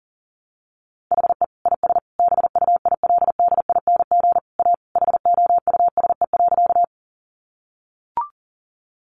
Morse code, heard as a keyed tone, about 40 wpm, 700 Hz. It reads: HE IS BUILDING A HOUSE.
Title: Sentences from Top 700 Words 40wpm